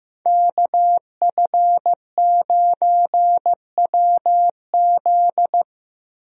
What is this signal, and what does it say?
Morse code, heard as a keyed tone, about 15 wpm, 700 Hz. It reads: KF9WZ
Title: USA Call Signs - 5 Characters - 15wpm